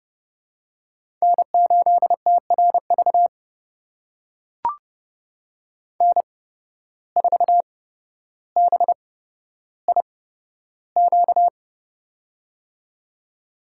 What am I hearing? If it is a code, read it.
Morse code, heard as a keyed tone, about 30 wpm, spaced as for 13 wpm, 700 Hz. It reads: N8TRV N4BIQ
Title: USA Call Signs - 5 Characters - 8x Character Spacing - 30wpm